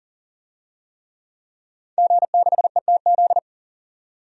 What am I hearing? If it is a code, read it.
Morse code, heard as a keyed tone, about 40 wpm, 700 Hz. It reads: G6ETZ
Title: DXCC-100 Call Signs - 5 Characters - 40wpm